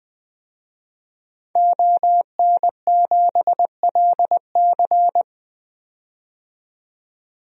Morse code, heard as a keyed tone, about 20 wpm, 700 Hz. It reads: ON7LC
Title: DXCC-100 Call Signs - 5 Characters - 20wpm